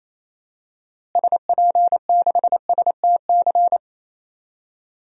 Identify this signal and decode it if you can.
Morse code, heard as a keyed tone, about 28 wpm, 700 Hz. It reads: SP6STC
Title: DXCC-100 Call Signs - 6 Characters - Repeated 3x - 28wpm